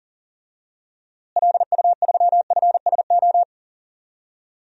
Morse code, heard as a keyed tone, about 40 wpm, 700 Hz. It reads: LU3FSO
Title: DXCC-100 Call Signs - 6 Characters - Repeated 2x - 40wpm